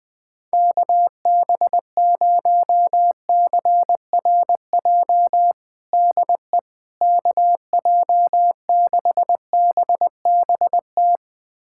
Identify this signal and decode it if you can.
Morse code, heard as a keyed tone, about 21 wpm, 700 Hz. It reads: KB0CRJ DE KJ6BBT
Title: USA Callsign DE Callsign - 6 Characters - 20wpm